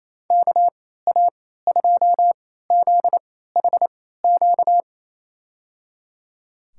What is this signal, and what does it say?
Morse code, heard as a keyed tone, about 27 wpm, spaced as for 16 wpm, 700 Hz. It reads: KA2ZHQ
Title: USA Call Signs - 6 Characters - 3x Character Spacing - 28wpm